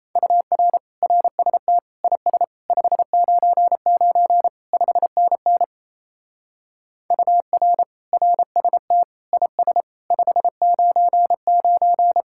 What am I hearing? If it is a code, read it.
Morse code, heard as a keyed tone, about 36 wpm, 700 Hz. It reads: UR RST IS 599 5NN UR RST IS 599